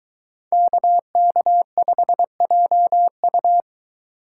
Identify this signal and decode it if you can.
Morse code, heard as a keyed tone, about 22 wpm, 700 Hz. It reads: KK5JU